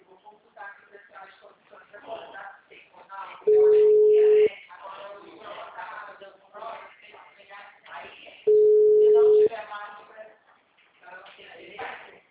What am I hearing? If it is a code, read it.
Ringback tone (Europe, 425 Hz), with speech.